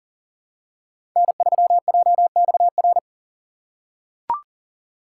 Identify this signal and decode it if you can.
Morse code, heard as a keyed tone, about 39 wpm, 700 Hz. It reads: N3JXR
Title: USA Call Signs - 5 Characters - 4x Character Spacing - 40wpm